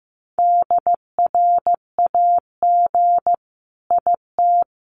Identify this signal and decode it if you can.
Morse code, heard as a keyed tone, about 15 wpm, 700 Hz. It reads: DRAG IT
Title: Sentences from Top 1500 Words 15wpm